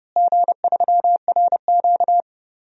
Morse code, heard as a keyed tone, about 32 wpm, 700 Hz. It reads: G3RQ
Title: DXCC-100 Call Signs - 4 Characters - Repeated 3x - 30wpm